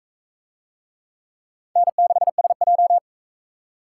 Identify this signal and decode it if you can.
Morse code, heard as a keyed tone, about 36 wpm, 700 Hz. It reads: N6SJ